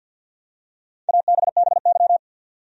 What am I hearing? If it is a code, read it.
Morse code, heard as a keyed tone, about 52 wpm, 700 Hz. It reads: ABBY
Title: USA Common Names - 50wpm